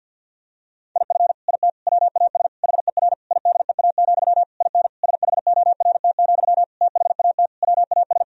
Morse code, heard as a keyed tone, about 49 wpm, 700 Hz. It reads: IF IT WAS HER IDEA, IN SHORT, THAT WAS
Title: Sentences from Top 1000 Words - Review Part 2 - 50wpm